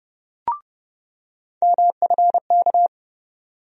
Morse code, heard as a keyed tone, about 30 wpm, 700 Hz. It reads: MFK
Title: USA Call Sign Postfixes - 3 Letters 30wpm